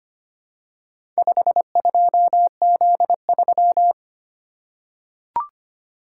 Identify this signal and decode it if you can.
Morse code, heard as a keyed tone, about 25 wpm, 700 Hz. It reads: 52Z3
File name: practice / 25wpm / Four Letter-Number Combinations - 25wpm